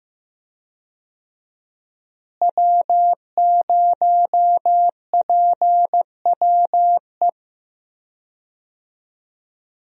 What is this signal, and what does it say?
Morse code, heard as a keyed tone, about 15 wpm, 700 Hz. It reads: W0PWE
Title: USA Call Signs - 5 Characters - Repeated 3x - 15wpm